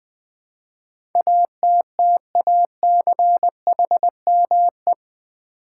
Morse code, heard as a keyed tone, about 20 wpm, 700 Hz. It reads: ATTACHME
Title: Build A Word - 4 Letters At A Time - 10-Letter Words - 20wpm